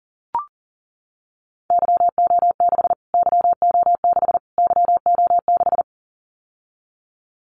Morse code, heard as a keyed tone, about 40 wpm, 700 Hz. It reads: YO6 YO6 YO6
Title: Callsign Prefixes for Top 25 DXCC - Repeated 3x - 40wpm